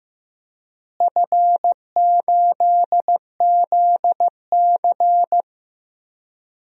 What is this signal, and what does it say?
Morse code, heard as a keyed tone, about 15 wpm, 700 Hz. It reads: F8ZC
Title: DXCC-100 Call Signs - 4 Characters - Repeated 3x - 15wpm